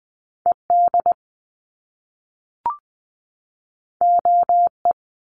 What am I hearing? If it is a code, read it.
Morse code, heard as a keyed tone, about 20 wpm, 700 Hz. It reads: ED OE
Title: Top Two Letter Combinations in Words 20wpm